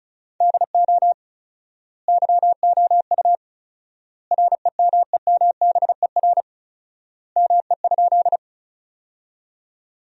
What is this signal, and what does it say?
Morse code, heard as a keyed tone, about 36 wpm, 700 Hz. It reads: DO YOU REMEMBER ME?
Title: 4x Word-Spacing - Sentences from Top 400 Words - 35wpm